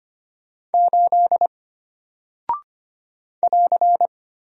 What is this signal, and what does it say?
Morse code, heard as a keyed tone, about 25 wpm, 700 Hz. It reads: 8 <AR>
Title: Single Character (Letter:Number:Punctuation:Prosign) Alt 25wpm